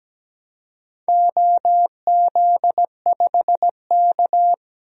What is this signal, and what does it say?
Morse code, heard as a keyed tone, about 17 wpm, 700 Hz. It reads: OZ5K
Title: DXCC-100 Call Signs - 4 Characters - Repeated 2x - 17wpm